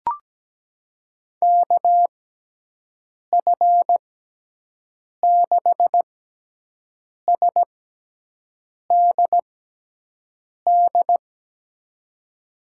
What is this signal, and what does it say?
Morse code, heard as a keyed tone, about 17 wpm, spaced as for 6 wpm, 700 Hz. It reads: KF6SDD
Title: USA Call Signs - 6 Characters - 6x Character Spacing - 17wpm